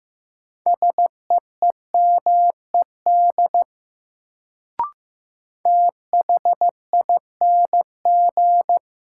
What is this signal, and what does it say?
Morse code, heard as a keyed tone, about 15 wpm, 700 Hz. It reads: SEEMED THING
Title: Sets of 3 Words - New Word Drill - 15wpm